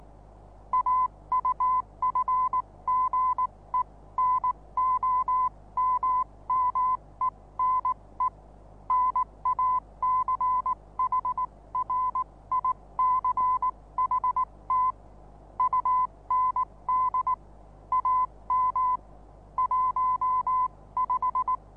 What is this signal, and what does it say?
Morse code, heard as a keyed tone, about 19 wpm, spaced as for 16 wpm, 1010 Hz. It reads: AUFGENOMMENE NACHRICHT UND AM 15
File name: repos / Morsekurs / DJ4UF Morsen26